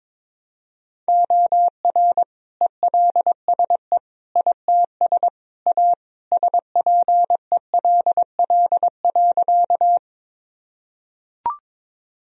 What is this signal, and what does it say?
Morse code, heard as a keyed tone, about 22 wpm, 700 Hz. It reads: OR ELSE ITS A SPELL.
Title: Sentences from Top 1000 Words - Review Part 1 - 22wpm